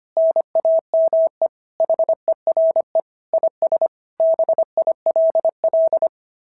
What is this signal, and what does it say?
Morse code, heard as a keyed tone, about 24 wpm, 650 Hz. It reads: NAME HERE IS BILL
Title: SOTA QSO Practice 25wpm